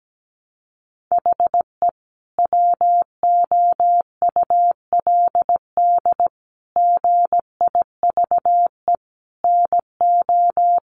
Morse code, heard as a keyed tone, about 17 wpm, 700 Hz. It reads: HE WOULD GIVE NO
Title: Sentences from Top 1200 Words 17wpm